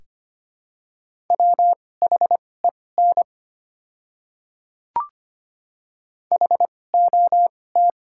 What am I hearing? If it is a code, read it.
Morse code, heard as a keyed tone, about 25 wpm, spaced as for 18 wpm, 690 Hz. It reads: WHEN HOT
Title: 2x Character Spacing - Top 100 Words - 25wpm